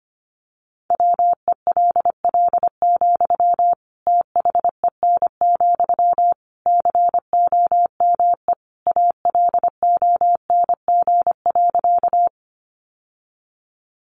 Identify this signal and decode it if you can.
Morse code, heard as a keyed tone, about 25 wpm, 700 Hz. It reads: WELL, THEN, COME ALONG.
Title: Sentences from Top 700 Words 25wpm